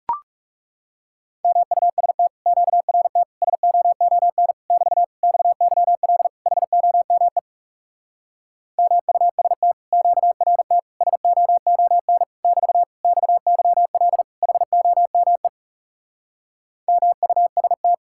Morse code, heard as a keyed tone, about 40 wpm, 700 Hz. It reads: MUST QRT SOON <BT> XYL HOME MUST QRT SOON <BT> XYL HOME MUST